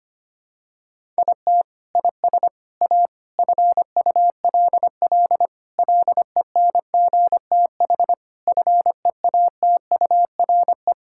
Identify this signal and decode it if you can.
Morse code, heard as a keyed tone, about 25 wpm, 700 Hz. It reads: IT IS A FULL LENGTH FEATURE
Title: Sentences from Top 1000 Words - Review Part 5 - 25wpm